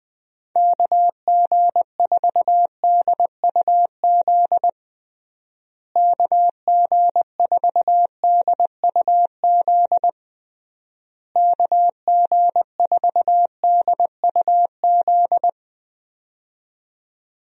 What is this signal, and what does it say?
Morse code, heard as a keyed tone, about 20 wpm, 700 Hz. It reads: KG4DUZ KG4DUZ KG4DUZ
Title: USA Call Signs - 6 Characters - Repeated 3x - 20wpm